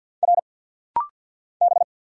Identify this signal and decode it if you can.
Morse code, heard as a keyed tone, about 50 wpm, 700 Hz. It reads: R B
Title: Single Letters - Rapid-Fire 50wpm